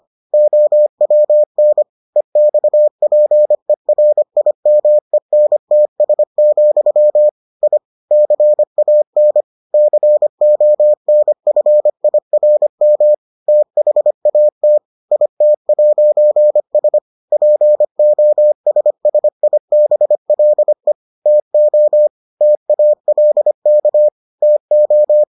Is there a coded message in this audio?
Yes: Morse code from 0.3 to 25.2 s, heard as a keyed tone, about 25 wpm, 600 Hz. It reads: OWN EXPERIMENTS, I CAN CONFIRM THAT IT'S POSSIBLE TO TALK TO